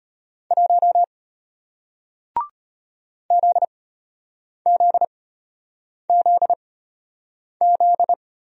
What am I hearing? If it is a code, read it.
Morse code, heard as a keyed tone, about 34 wpm, spaced as for 8 wpm, 700 Hz. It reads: 1 ZZZZ